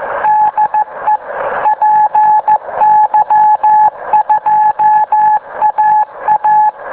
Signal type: Morse code